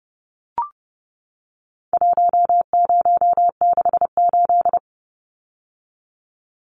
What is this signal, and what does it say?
Morse code, heard as a keyed tone, about 30 wpm, 700 Hz. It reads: 1068
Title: Numbers - 4-Digit 30wpm